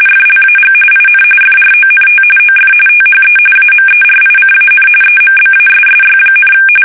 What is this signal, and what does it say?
RTTY, about 45.45 baud. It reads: FLAG IN FRONT OF THE AMERICAN EMBASSY,
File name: fsk850 / usnavy1